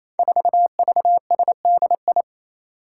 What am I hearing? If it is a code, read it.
Morse code, heard as a keyed tone, about 27 wpm, 700 Hz. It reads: 4VSDI
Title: Five Challenging Letter-Number Combinations 28wpm